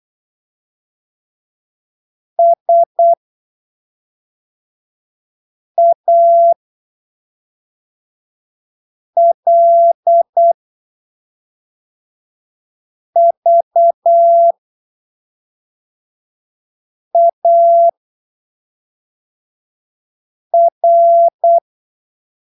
Morse code, heard as a keyed tone, about 8 wpm, spaced as for 3 wpm, 680 Hz. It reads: SALVAR